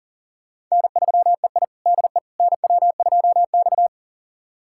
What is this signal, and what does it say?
Morse code, heard as a keyed tone, about 39 wpm, 700 Hz. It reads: N3EI DE NW2X
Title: USA Callsign DE Callsign - 4 Characters - 40wpm